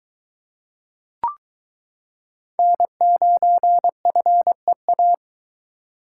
Morse code, heard as a keyed tone, about 22 wpm, 700 Hz. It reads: N9FEA